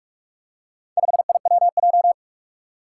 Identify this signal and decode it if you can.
Morse code, heard as a keyed tone, about 44 wpm, 700 Hz. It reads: 5IWJ
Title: Four Challenging Letter-Number Combinations 45wpm